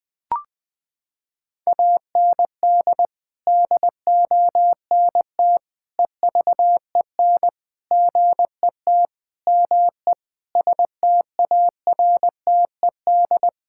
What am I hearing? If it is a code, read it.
Morse code, heard as a keyed tone, about 20 wpm, 700 Hz. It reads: AND DONT EVEN GET ME STARTED
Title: Sets of 8 Words - 20wpm